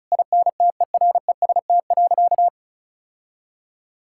Morse code, heard as a keyed tone, about 36 wpm, 700 Hz. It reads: INTEREST.
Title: Sentences from Top 1000 Words - Review Part 6 - 35wpm